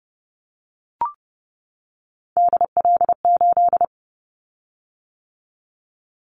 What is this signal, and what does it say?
Morse code, heard as a keyed tone, about 30 wpm, 700 Hz. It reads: DL8
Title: Callsign Prefixes for Top 50 DXCC - 30wpm